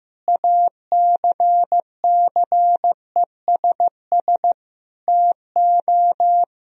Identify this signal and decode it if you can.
Morse code, heard as a keyed tone, about 15 wpm, 700 Hz. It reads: ACCESS TO
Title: Sentences from Top 1000 Words - Review Part 4 - 15wpm